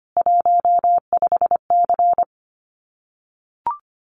Morse code, heard as a keyed tone, about 25 wpm, 700 Hz. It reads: 15C
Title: Common QSO Phrases 25wpm